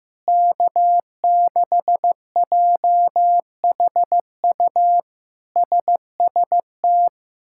Morse code, heard as a keyed tone, about 15 wpm, 700 Hz. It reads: K6JHU SST
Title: K1USN SST QSO Practice - 15wpm